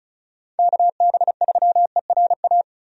Morse code, heard as a keyed tone, about 34 wpm, 700 Hz. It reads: KB3ERA
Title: USA Call Signs - 6 Characters - 35wpm